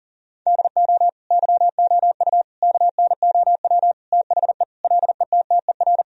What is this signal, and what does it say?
Morse code, heard as a keyed tone, about 40 wpm, 700 Hz. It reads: DO YOU KNOW THE LETTER
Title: Sentences from Top 1000 Words - Review Part 2 - 40wpm